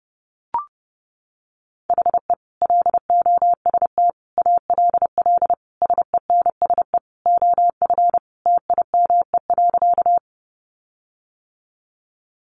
Morse code, heard as a keyed tone, about 30 wpm, 700 Hz. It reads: HE LOST ALL SENSE OF TIME.
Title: Sentences from Top 1400 Words 30wpm